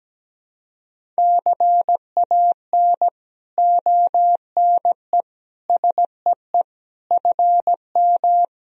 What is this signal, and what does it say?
Morse code, heard as a keyed tone, about 17 wpm, 700 Hz. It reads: CAN ONE SEE FM
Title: Sentences from Top 100 Words and Abbreviations 17wpm